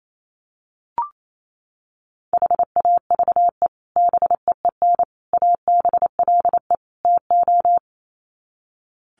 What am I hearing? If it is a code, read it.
Morse code, heard as a keyed tone, about 28 wpm, 700 Hz. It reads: HAVE BEEN ABLE TO
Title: Sets of 4 Words 28wpm